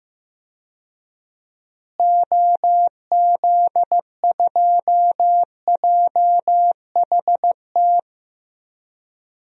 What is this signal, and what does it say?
Morse code, heard as a keyed tone, about 15 wpm, 700 Hz. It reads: OZ2JHT
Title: DXCC-100 Call Signs - 6 Characters - Repeated 2x - 15wpm